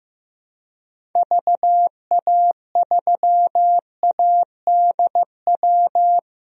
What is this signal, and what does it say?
Morse code, heard as a keyed tone, about 15 wpm, 700 Hz. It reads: VA3ADW
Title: DXCC-100 Call Signs - 6 Characters - 15wpm